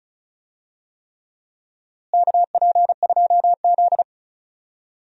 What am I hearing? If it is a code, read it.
Morse code, heard as a keyed tone, about 36 wpm, 700 Hz. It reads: KP2Z
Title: DXCC-100 Call Signs - 4 Characters - Repeated 3x - 35wpm